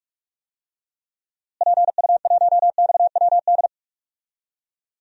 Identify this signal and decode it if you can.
Morse code, heard as a keyed tone, about 42 wpm, 700 Hz. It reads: PU1XWD
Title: DXCC-100 Call Signs - 6 Characters - Repeated 2x - 45wpm